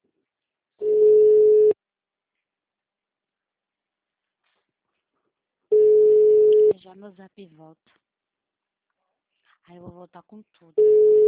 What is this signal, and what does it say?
Ringback tone (Europe, 425 Hz), with speech.